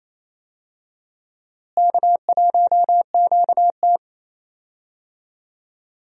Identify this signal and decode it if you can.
Morse code, heard as a keyed tone, about 28 wpm, 700 Hz. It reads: K1QT
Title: USA Call Signs - 4 Characters - Repeated 2x - 28wpm